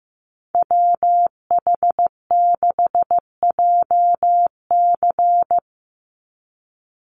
Morse code, heard as a keyed tone, about 15 wpm, 700 Hz. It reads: WH6JC
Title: USA Call Signs 15wpm